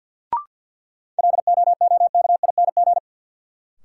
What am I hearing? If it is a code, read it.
Morse code, heard as a keyed tone, about 53 wpm, 700 Hz. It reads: LOOKING
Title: Sentences from Top 1000 Words - Review Part 6 - New Word Drill - 50wpm